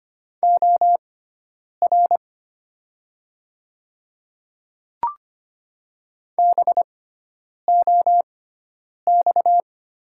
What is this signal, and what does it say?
Morse code, heard as a keyed tone, about 25 wpm, spaced as for 9 wpm, 700 Hz. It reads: OR BOX
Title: 6x Character Spacing - Top 500 Words - 25wpm